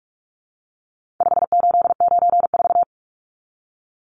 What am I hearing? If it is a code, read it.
Morse code, heard as a keyed tone, about 44 wpm, 700 Hz. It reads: 5894
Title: Numbers - 4-Digit 45wpm